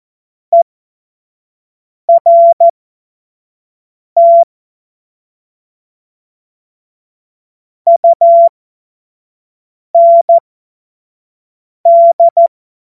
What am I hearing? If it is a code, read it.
Morse code, heard as a keyed tone, about 14 wpm, spaced as for 5 wpm, 680 Hz. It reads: ERT UND
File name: Morsekurs / Witze / Witz 07. Dezember 2021, 25 F70